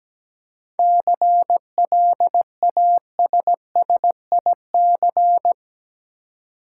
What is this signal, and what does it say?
Morse code, heard as a keyed tone, about 17 wpm, 700 Hz. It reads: CLASSIC
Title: Seven Letter Words - 17wpm